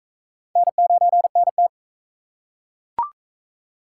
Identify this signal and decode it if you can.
Morse code, heard as a keyed tone, about 40 wpm, 700 Hz. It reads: N9NT